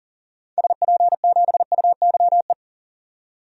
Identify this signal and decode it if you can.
Morse code, heard as a keyed tone, about 40 wpm, 700 Hz. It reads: SP7UYE
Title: DXCC-100 Call Signs - 6 Characters - 40wpm